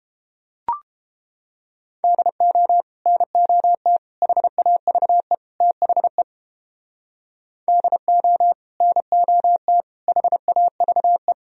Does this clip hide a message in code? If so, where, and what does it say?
Yes: Morse code from 2.0 to 11.3 s, heard as a keyed tone, about 29 wpm, 700 Hz. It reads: DO NOT HAVE THE DO NOT HAVE